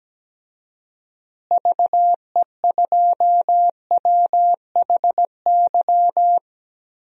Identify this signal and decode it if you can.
Morse code, heard as a keyed tone, about 17 wpm, 700 Hz. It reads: VE2WHY
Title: DXCC-100 Call Signs - 6 Characters - Repeated 3x - 17wpm